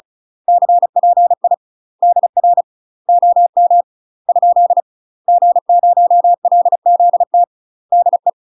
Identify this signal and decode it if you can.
Morse code, heard as a keyed tone, about 36 wpm, 700 Hz. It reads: CPI DR OM ? G0LZT DE
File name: UK MASTER QSOs TRAINING / 35WPM / UK MASTER QSOs 0023